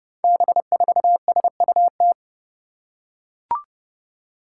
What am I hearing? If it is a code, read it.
Morse code, heard as a keyed tone, about 30 wpm, 700 Hz. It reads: B4SUT
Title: Five Challenging Letter-Number Combinations 30wpm